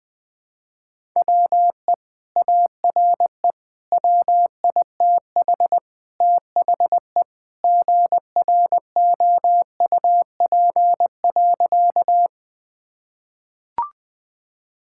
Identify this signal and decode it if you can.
Morse code, heard as a keyed tone, about 20 wpm, 700 Hz. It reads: WE ARE WITH THE GROUP.
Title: Sentences with 2-Word Phrases - 20wpm